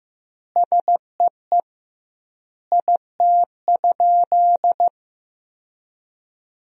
Morse code, heard as a keyed tone, about 15 wpm, 700 Hz. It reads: SEE IT?
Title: 2x Word-Spacing - Sentences from Top 100 Words - 15wpm